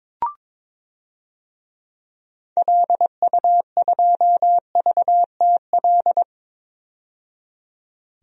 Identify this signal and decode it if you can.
Morse code, heard as a keyed tone, about 22 wpm, 700 Hz. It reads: LU2VTL
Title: DXCC-100 Call Signs - 6 Characters - 22wpm